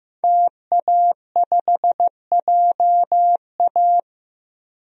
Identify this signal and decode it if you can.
Morse code, heard as a keyed tone, about 15 wpm, 700 Hz. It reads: TA5JA
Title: DXCC-100 Call Signs - 5 Characters - Repeated 2x - 15wpm